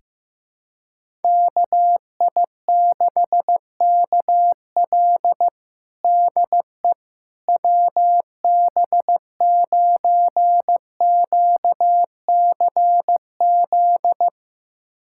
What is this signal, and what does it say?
Morse code, heard as a keyed tone, about 15 wpm, 700 Hz. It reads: KI6KL DE WB9QCZ
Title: USA Callsign DE Callsign 15wpm